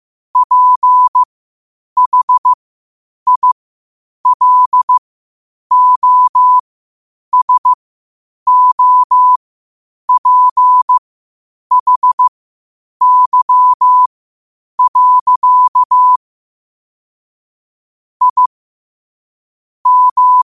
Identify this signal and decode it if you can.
Morse code, heard as a keyed tone, about 15 wpm, spaced as for 8 wpm, 1000 Hz. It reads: PHILOSOPHY. I M
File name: qotdfiles / 8wpm